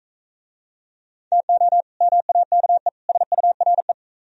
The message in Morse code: TO MAKE SURE

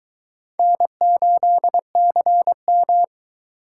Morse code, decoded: N8CM

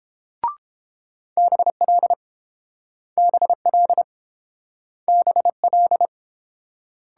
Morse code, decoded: BL BL BL

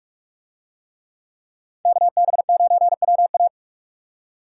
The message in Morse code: KB9WA

